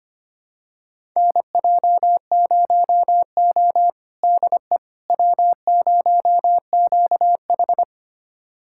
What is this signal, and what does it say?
Morse code, heard as a keyed tone, about 25 wpm, 700 Hz. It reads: NJ0O DE W0QH
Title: USA Callsign DE Callsign - 4 Characters - 25wpm